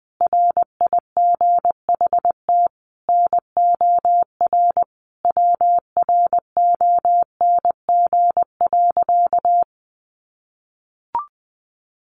Morse code, heard as a keyed tone, about 20 wpm, 700 Hz. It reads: RIGHT NOR WRONG.
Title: Sentences from Top 1500 Words 20wpm